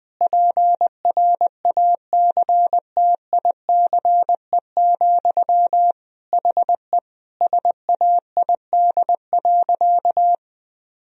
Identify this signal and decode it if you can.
Morse code, heard as a keyed tone, about 21 wpm, 700 Hz. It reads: PRACTICE, HE SAID.
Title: Sentences from Top 1000 Words - Review Part 4 - 20wpm